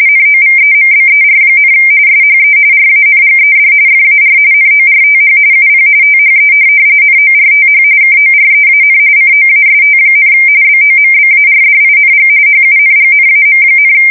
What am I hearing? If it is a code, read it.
RTTY, about 45.45 baud. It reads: NOVEMBER, SHE RANGED FAR NORTH UP THE NORTH KOREAN COAST, BOMBARDING HUNGNAM,
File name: fsk170 / USS-New-Jersey